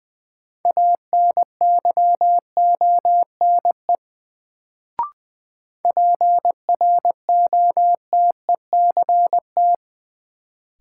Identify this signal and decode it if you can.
Morse code, heard as a keyed tone, about 20 wpm, 700 Hz. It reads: ANYONE PROTECT
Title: Sentences from Top 900 Words - New Word Drill - 20wpm